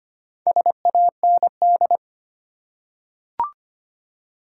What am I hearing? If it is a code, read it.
Morse code, heard as a keyed tone, about 25 wpm, 700 Hz. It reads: SAND
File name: practice / 25wpm / Easy Words with Hints - 25wpm